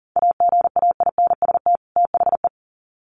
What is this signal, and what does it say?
Morse code, heard as a keyed tone, about 40 wpm, 700 Hz. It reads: AGAINST THE
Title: Sets of 2 Words 40wpm